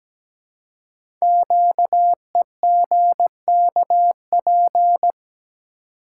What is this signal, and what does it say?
Morse code, heard as a keyed tone, about 17 wpm, 700 Hz. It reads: QEGKP
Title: Code Groups - 5 Letters - 17wpm